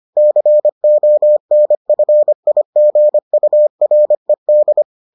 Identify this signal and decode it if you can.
Morse code, heard as a keyed tone, about 25 wpm, 600 Hz. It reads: CONFIGURED